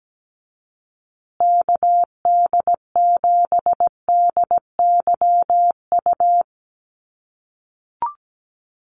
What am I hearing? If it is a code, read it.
Morse code, heard as a keyed tone, about 17 wpm, 700 Hz. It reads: KD7DYU
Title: USA Call Signs Repeated 3x 17wpm